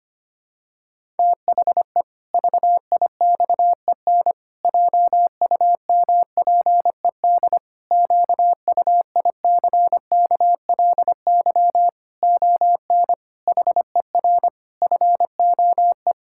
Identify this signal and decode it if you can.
Morse code, heard as a keyed tone, about 25 wpm, 700 Hz. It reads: THE VIXEN JUMPED QUICKLY ON HER FOE
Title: Pangram Challenge v2 - 25wpm